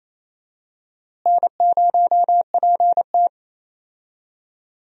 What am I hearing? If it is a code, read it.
Morse code, heard as a keyed tone, about 28 wpm, 700 Hz. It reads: N0PT
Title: USA Call Signs - 4 Characters - Repeated 3x - 28wpm